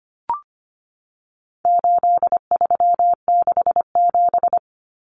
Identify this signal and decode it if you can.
Morse code, heard as a keyed tone, about 25 wpm, 700 Hz. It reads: 8367
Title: Common Cut Numbers - 4-Digit 25wpm